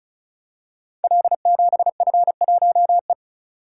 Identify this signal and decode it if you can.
Morse code, heard as a keyed tone, about 36 wpm, 700 Hz. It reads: L7F1E
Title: Code Groups - 5 Letter:Numbers - 35wpm